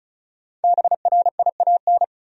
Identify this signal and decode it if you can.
Morse code, heard as a keyed tone, about 34 wpm, 700 Hz. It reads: BRIAN
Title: USA Common Names - 35wpm